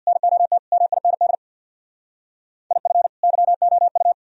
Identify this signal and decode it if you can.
Morse code, heard as a keyed tone, about 67 wpm, 700 Hz. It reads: NOT MIND IF YOU